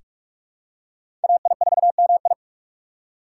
Morse code, heard as a keyed tone, about 45 wpm, 700 Hz. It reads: AI4MI
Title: K1USN SST QSO Practice - 45wpm